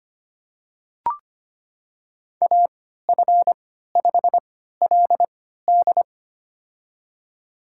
Morse code, heard as a keyed tone, about 25 wpm, spaced as for 14 wpm, 700 Hz. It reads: AF5LD
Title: USA Call Signs - 3x Character Spacing - 25wpm